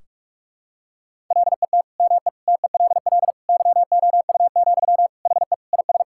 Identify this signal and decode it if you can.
Morse code, heard as a keyed tone, about 46 wpm, 700 Hz. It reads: LET ME TELL YOU, HE IS
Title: Sentences from Top 1600 Words 45wpm